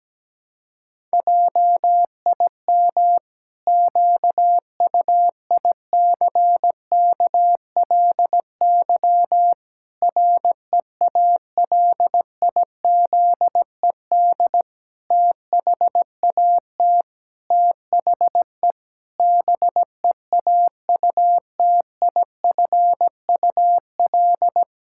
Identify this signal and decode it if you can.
Morse code, heard as a keyed tone, about 17 wpm, 700 Hz. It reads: JIM QUICKLY REALIZED THAT THE BEAUTIFUL
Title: Pangram Challenge v2 - 17wpm